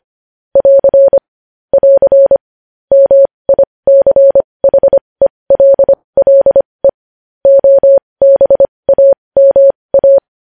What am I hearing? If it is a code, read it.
Morse code, heard as a keyed tone, about 25 wpm, spaced as for 20 wpm, 550 Hz. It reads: <AR> <AR> MICHELLE OBAMA
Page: News Headlines in Morse Code at 20 WPM